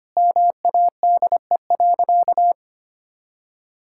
Morse code, heard as a keyed tone, about 25 wpm, 700 Hz. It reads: MADE.
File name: practice / 25wpm / Sentences with 4-Word Phrases - 25wpm